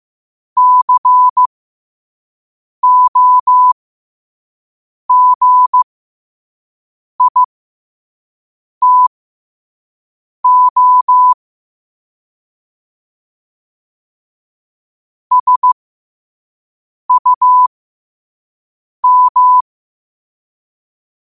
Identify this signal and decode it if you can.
Morse code, heard as a keyed tone, about 15 wpm, spaced as for 5 wpm, 1000 Hz. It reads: COGITO SUM
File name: qotdfiles / 5wpm